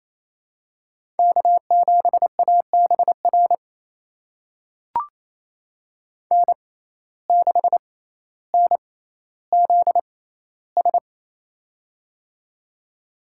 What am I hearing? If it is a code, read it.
Morse code, heard as a keyed tone, about 28 wpm, spaced as for 14 wpm, 700 Hz. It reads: K7ABR N6NZS
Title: USA Call Signs - 5 Characters - 6x Character Spacing - 28wpm